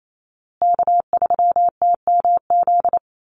Morse code, heard as a keyed tone, about 28 wpm, 700 Hz. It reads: K3TMZ
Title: USA Call Signs Repeated 2x 28wpm